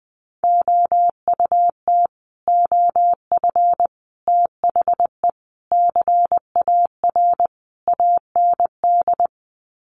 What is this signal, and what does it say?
Morse code, heard as a keyed tone, about 20 wpm, 700 Hz. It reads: OUT OF THE CAR AND
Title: Sets of 5 Words Encore 20wpm